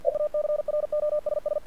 Morse code, heard as a keyed tone, about 48 wpm, 600 Hz. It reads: E ACROSS
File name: archived / CRSnet